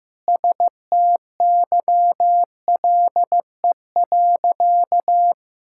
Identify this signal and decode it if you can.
Morse code, heard as a keyed tone, about 14 wpm, 700 Hz. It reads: STYLE.
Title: Sentences from Top 1000 Words - Review Part 5 - 15wpm